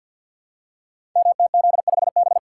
Morse code, heard as a keyed tone, about 50 wpm, 700 Hz. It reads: MTZ5B
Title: Code Groups - 5 Letter:Numbers - 50wpm